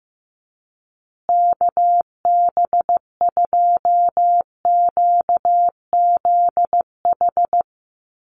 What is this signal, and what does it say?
Morse code, heard as a keyed tone, about 15 wpm, 700 Hz. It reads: KB2QZH
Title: USA Call Signs Repeated 2x 15wpm